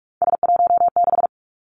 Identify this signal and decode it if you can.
Morse code, heard as a keyed tone, about 46 wpm, 700 Hz. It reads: S16
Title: Three Challenging Letter-Number Combinations 45wpm